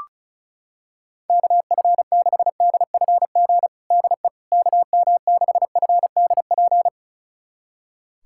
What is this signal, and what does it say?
Morse code, heard as a keyed tone, about 36 wpm, 700 Hz. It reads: KF6DFG DE KM6FDP
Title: USA Callsign DE Callsign - 6 Characters - 35wpm